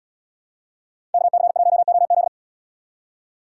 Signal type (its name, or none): Morse code